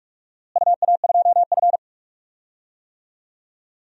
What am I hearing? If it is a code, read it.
Morse code, heard as a keyed tone, about 46 wpm, 700 Hz. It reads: UA2F
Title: DXCC-100 Call Signs - 4 Characters - Repeated 3x - 45wpm